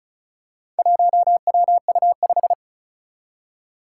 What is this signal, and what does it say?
Morse code, heard as a keyed tone, about 34 wpm, 700 Hz. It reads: 1WU5
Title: Four Challenging Letter-Number Combinations 35wpm